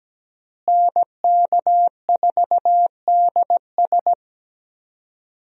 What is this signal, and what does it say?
Morse code, heard as a keyed tone, about 17 wpm, 700 Hz. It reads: NK4DS
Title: USA Call Signs - 5 Characters - Repeated 2x - 17wpm